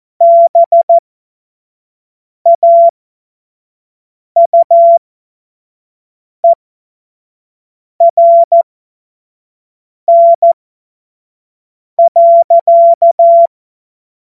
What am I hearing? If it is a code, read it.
Morse code, heard as a keyed tone, about 14 wpm, spaced as for 5 wpm, 680 Hz. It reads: BAUERN.
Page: cw-witze